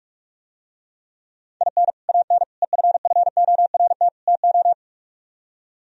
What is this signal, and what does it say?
Morse code, heard as a keyed tone, about 47 wpm, 700 Hz. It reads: IN AN EFFORT TO